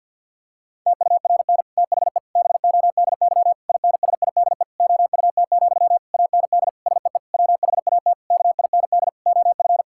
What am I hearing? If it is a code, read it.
Morse code, heard as a keyed tone, about 50 wpm, 700 Hz. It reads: TURN THE BODY INSIDE OUT, AND SEE WHAT KIND OF
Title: Sentences from Top 1000 Words - Review Part 6 - 50wpm